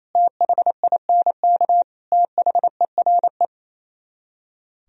Morse code, heard as a keyed tone, about 35 wpm, 700 Hz. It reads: THINK THERE
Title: Speed-Racing - Sets of 3 Words 28wpm